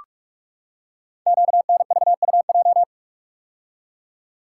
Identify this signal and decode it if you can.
Morse code, heard as a keyed tone, about 44 wpm, 700 Hz. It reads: QNVUJ
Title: Code Groups - 5 Letters - 45wpm